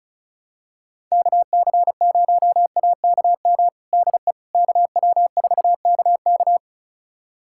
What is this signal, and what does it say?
Morse code, heard as a keyed tone, about 36 wpm, 700 Hz. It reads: KC0AKM DE KW4KK
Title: USA Callsign DE Callsign 35wpm